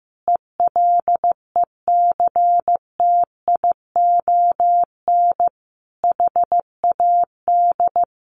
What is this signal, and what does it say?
Morse code, heard as a keyed tone, about 15 wpm, 700 Hz. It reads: ELECTION HAD
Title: A Sound Of Thunder 15wpm